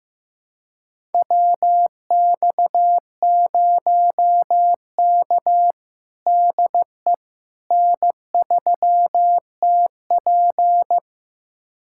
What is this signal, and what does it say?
Morse code, heard as a keyed tone, about 15 wpm, 700 Hz. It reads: WX0K DE N3TP
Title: USA Callsign DE Callsign - 4 Characters - 15wpm